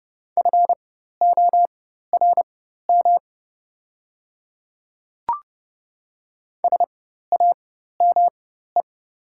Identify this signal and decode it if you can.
Morse code, heard as a keyed tone, about 30 wpm, spaced as for 14 wpm, 700 Hz. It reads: FORM SAME
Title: 4x Character Spacing - Top 300 Words - 30wpm